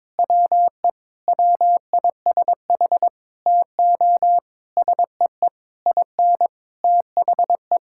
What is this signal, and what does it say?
Morse code, heard as a keyed tone, about 22 wpm, 700 Hz. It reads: WE WISH TO SEE IN THE
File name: Sentences with 9-Word Phrases - 22wpm